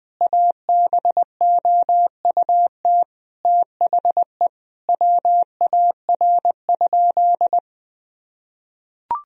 Morse code, heard as a keyed tone, about 20 wpm, 700 Hz. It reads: ABOUT THE WAR?
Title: Sentences from Top 1000 Words - Review Part 4 - 20wpm